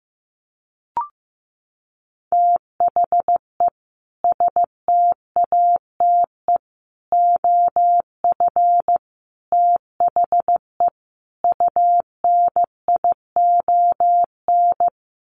Morse code, heard as a keyed tone, about 15 wpm, 700 Hz. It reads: THE STATE OF THE UNION
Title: Sets of 5 Words Encore 15wpm